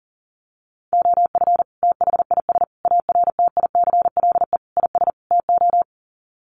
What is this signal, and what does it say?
Morse code, heard as a keyed tone, about 40 wpm, 700 Hz. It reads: OF THIS ARTICLE IS TO
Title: Sets of 5 Words Encore2 40wpm